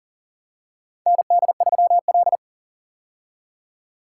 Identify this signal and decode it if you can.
Morse code, heard as a keyed tone, about 40 wpm, 700 Hz. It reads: ND3L
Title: K1USN SST QSO Practice - 40wpm